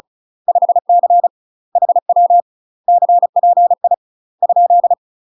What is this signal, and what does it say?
Morse code, heard as a keyed tone, about 34 wpm, 700 Hz. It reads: 5C HW CPI ?
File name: UK MASTER QSOs TRAINING / 35WPM / UK MASTER QSOs 0043